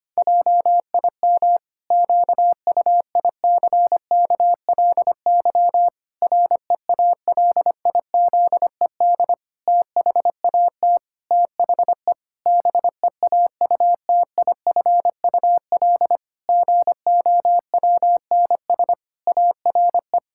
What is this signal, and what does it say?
Morse code, heard as a keyed tone, about 25 wpm, 700 Hz. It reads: JIM QUICKLY REALIZED THAT THE BEAUTIFUL GOWNS ARE
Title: Pangram Challenge v2 - 25wpm